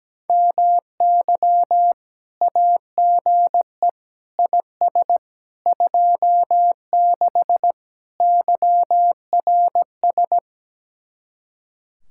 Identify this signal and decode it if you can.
Morse code, heard as a keyed tone, about 17 wpm, 700 Hz. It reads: MY AGE IS 26 YRS
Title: Speed-Racing Common QSO Phrases - 17wpm